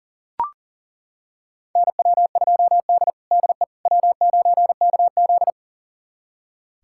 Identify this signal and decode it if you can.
Morse code, heard as a keyed tone, about 40 wpm, 700 Hz. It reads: NW2D DE W9KZ
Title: USA Callsign DE Callsign - 4 Characters - 40wpm